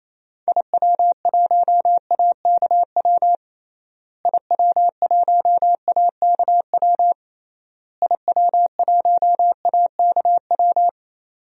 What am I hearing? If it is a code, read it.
Morse code, heard as a keyed tone, about 28 wpm, 700 Hz. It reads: IW1AKW IW1AKW IW1AKW
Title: DXCC-100 Call Signs - 6 Characters - Repeated 3x - 28wpm